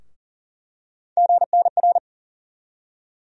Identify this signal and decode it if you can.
Morse code, heard as a keyed tone, about 40 wpm, 700 Hz. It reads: GNR
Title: USA Call Sign Postfixes - 3 Letters 40wpm